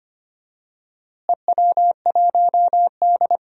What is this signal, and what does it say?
Morse code, heard as a keyed tone, about 25 wpm, 700 Hz. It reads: EW1D
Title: DXCC-100 Call Signs - 4 Characters - Repeated 3x - 25wpm